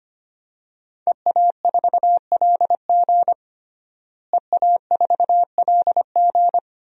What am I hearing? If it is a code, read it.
Morse code, heard as a keyed tone, about 25 wpm, 700 Hz. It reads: EA4LG EA4LG
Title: DXCC-100 Call Signs - 5 Characters - Repeated 3x - 25wpm